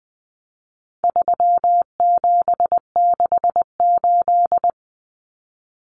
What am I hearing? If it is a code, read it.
Morse code, heard as a keyed tone, about 20 wpm, 700 Hz. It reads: 3768
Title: Numbers - 4-Digit 20wpm